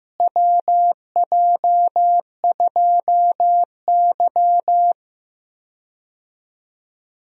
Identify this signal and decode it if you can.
Morse code, heard as a keyed tone, about 15 wpm, 700 Hz. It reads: WJ2Y
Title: USA Call Signs - 4 Characters - Repeated 2x - 15wpm